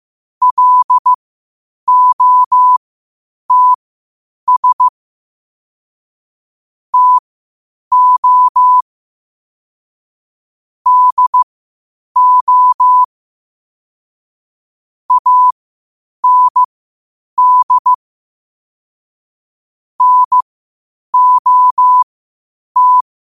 Morse code, heard as a keyed tone, about 15 wpm, spaced as for 8 wpm, 1000 Hz. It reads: LOTS TO DO AND NOT
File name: qotdfiles / 8wpm